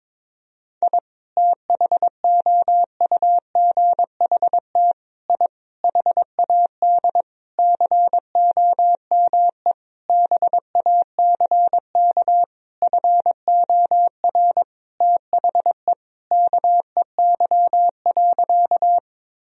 Morse code, heard as a keyed tone, about 22 wpm, 700 Hz. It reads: I THOUGHT I HAD COME BACK FOR THE KEY.
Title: Sentences from Top 1000 Words - Review Part 4 - 22wpm